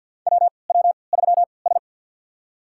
Morse code, heard as a keyed tone, about 48 wpm, spaced as for 28 wpm, 700 Hz. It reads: WW3S
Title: USA Call Signs - 4 Characters - 3x Character Spacing - 50wpm